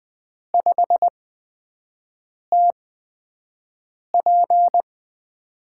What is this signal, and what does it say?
Morse code, heard as a keyed tone, about 20 wpm, spaced as for 5 wpm, 700 Hz. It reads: 5TP
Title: USA Call Signs - 5 Characters - 8x Character Spacing - 20wpm